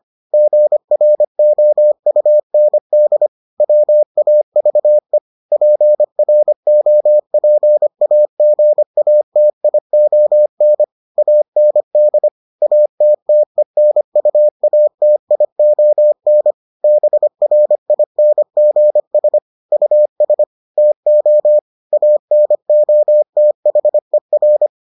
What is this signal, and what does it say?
Morse code, heard as a keyed tone, about 25 wpm, 600 Hz. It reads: GROUND WAVE PROPAGATION AND ATTENUATION BRINGS US TO ANOTHER